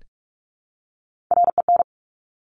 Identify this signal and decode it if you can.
Morse code, heard as a keyed tone, about 46 wpm, 700 Hz. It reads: REN
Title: Top Three Letter Combinations in Words 45wpm